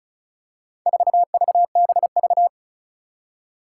Morse code, heard as a keyed tone, about 35 wpm, 700 Hz. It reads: 4VBV
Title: Four Challenging Letter-Number Combinations 35wpm